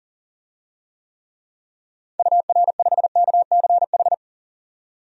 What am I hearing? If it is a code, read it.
Morse code, heard as a keyed tone, about 40 wpm, 700 Hz. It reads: UR5KCH
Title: DXCC-100 Call Signs - 6 Characters - 40wpm